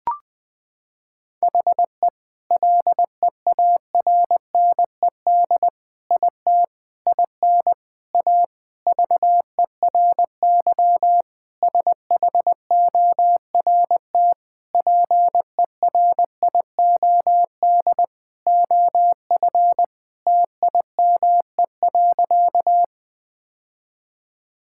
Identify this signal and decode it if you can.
Morse code, heard as a keyed tone, about 21 wpm, 700 Hz. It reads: HE LEARNED IT IN A VERY SHORT PERIOD OF TIME.
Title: Sentences with 7-Word Phrases - 20wpm